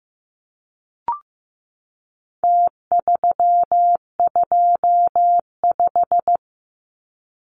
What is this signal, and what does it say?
Morse code, heard as a keyed tone, about 15 wpm, 700 Hz. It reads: T325
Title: Numbers - Contest Style - 4-Digit 15wpm